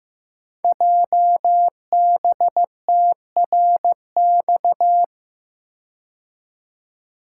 Morse code, heard as a keyed tone, about 15 wpm, 700 Hz. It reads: JBTRX
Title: Code Groups - 5 Letters - 15wpm